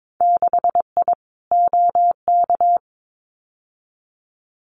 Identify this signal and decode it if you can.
Morse code, heard as a keyed tone, about 22 wpm, 700 Hz. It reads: 6I OK
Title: Winter Field Day Exchanges Encore 22wpm